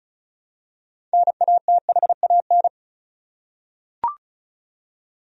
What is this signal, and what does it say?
Morse code, heard as a keyed tone, about 34 wpm, 700 Hz. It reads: NATHAN
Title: USA Common Names - 35wpm